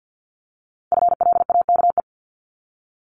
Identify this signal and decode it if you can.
Morse code, heard as a keyed tone, about 50 wpm, 700 Hz. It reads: FLAKE
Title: Multi-letter Phonemes Repeated 3x 50wpm